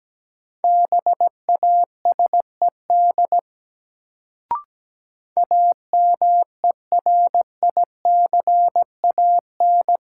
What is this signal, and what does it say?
Morse code, heard as a keyed tone, about 17 wpm, 700 Hz. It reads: BASED AMERICAN
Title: Sets of 2 Words - New Word Drill - 17wpm